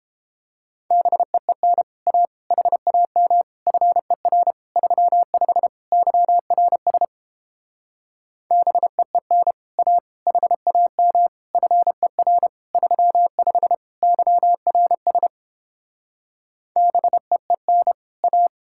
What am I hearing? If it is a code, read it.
Morse code, heard as a keyed tone, about 29 wpm, 700 Hz. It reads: BEEN A HAM FER 35 YRS BEEN A HAM FER 35 YRS BEEN A